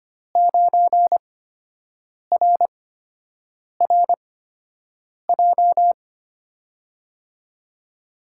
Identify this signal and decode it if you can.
Morse code, heard as a keyed tone, about 25 wpm, spaced as for 7 wpm, 700 Hz. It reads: 9RRJ
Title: USA Call Signs - 5 Characters - 8x Character Spacing - 25wpm